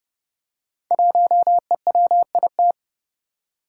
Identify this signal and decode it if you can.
Morse code, heard as a keyed tone, about 30 wpm, 700 Hz. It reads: 1EWIT
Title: Five Challenging Letter-Number Combinations 30wpm